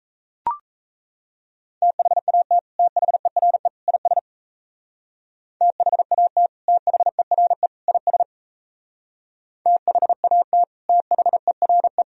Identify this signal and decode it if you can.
Morse code, heard as a keyed tone, about 35 wpm, 700 Hz. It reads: THAT THERE IS THAT THERE IS THAT THERE
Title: Speed-Racing - Sets of 3 Words 28wpm